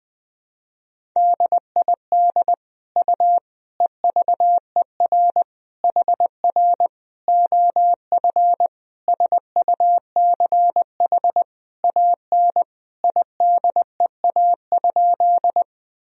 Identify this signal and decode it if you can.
Morse code, heard as a keyed tone, about 20 wpm, 700 Hz. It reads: DID U EVER HR OF SUCH AN IDEA?
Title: Sentences from Top 400 Words and Abbreviations 20wpm